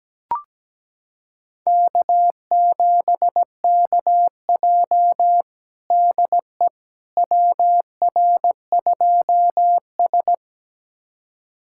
Morse code, heard as a keyed tone, about 17 wpm, 700 Hz. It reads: K7KJ DE WR2S
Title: USA Callsign DE Callsign - 4 Characters - 17wpm